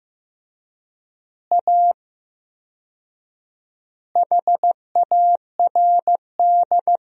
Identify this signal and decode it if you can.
Morse code, heard as a keyed tone, about 15 wpm, 700 Hz. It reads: A HARD
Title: 4x Word-Spacing - Sentences from Top 300 Words - 15wpm